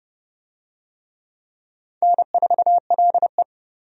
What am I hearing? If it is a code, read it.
Morse code, heard as a keyed tone, about 29 wpm, 700 Hz. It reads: N4LE
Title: USA Call Signs - 4 Characters - 30wpm